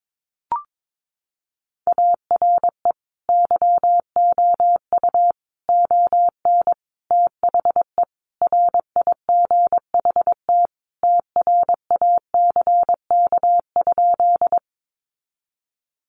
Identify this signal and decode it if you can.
Morse code, heard as a keyed tone, about 22 wpm, 700 Hz. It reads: ARE YOU ON THE RIGHT TRACK?
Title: Sentences from Top 1100 Words 22wpm